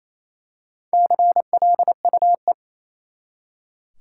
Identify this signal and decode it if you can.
Morse code, heard as a keyed tone, about 27 wpm, 700 Hz. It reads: CLUE
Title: Four Letter Words - 28wpm